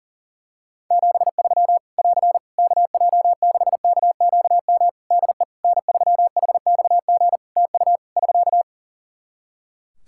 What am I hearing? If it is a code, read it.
Morse code, heard as a keyed tone, about 40 wpm, 700 Hz. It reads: 73 <AR> KJ6KQM DE N3HXG TU <SK>
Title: Practice QSO - Exchange 4 - Ending - 40wpm